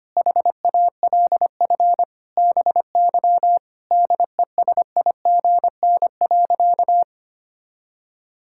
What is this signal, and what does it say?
Morse code, heard as a keyed tone, about 25 wpm, 700 Hz. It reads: HALF BY DESIGN.
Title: Sentences from Top 1000 Words - Review Part 5 - 25wpm